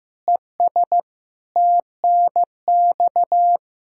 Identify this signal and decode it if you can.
Morse code, heard as a keyed tone, about 15 wpm, 700 Hz. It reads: ES TNX
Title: Practice QSO - Exchange 2 - Info - 15wpm